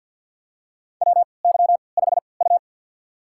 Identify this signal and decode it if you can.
Morse code, heard as a keyed tone, about 51 wpm, spaced as for 28 wpm, 700 Hz. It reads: WY5U
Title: USA Call Signs - 3x Character Spacing - 50wpm